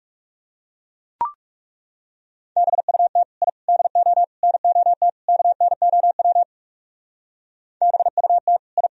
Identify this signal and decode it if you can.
Morse code, heard as a keyed tone, about 39 wpm, 700 Hz. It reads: BUT I DO NOT KNOW BUT I